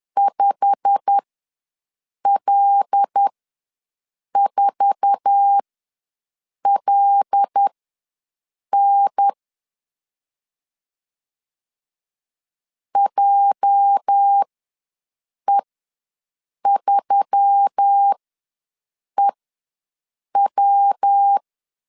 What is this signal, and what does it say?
Morse code, heard as a keyed tone, about 10 wpm, spaced as for 6 wpm, 800 Hz. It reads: 5L4LN JE3EW
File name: LEZIONI / 9 2867 ws 10wpm 4215